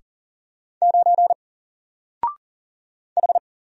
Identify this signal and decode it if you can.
Morse code, heard as a keyed tone, about 40 wpm, 700 Hz. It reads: 9 H
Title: Single Character (Letter:Number:Punctuation:Prosign) Alt 40wpm